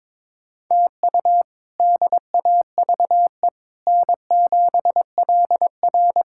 Morse code, heard as a keyed tone, about 21 wpm, 700 Hz. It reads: TU DAVE N7LR
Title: K1USN SST QSO Practice - 22wpm